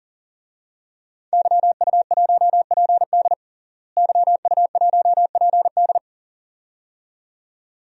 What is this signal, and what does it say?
Morse code, heard as a keyed tone, about 40 wpm, 700 Hz. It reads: YU1PD YU1PD
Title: DXCC-100 Call Signs - 5 Characters - Repeated 2x - 40wpm